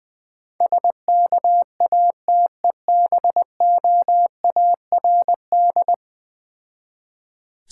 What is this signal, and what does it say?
Morse code, heard as a keyed tone, about 20 wpm, 700 Hz. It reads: SKATEBOARD
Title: Difficult Words with Hints - 20wpm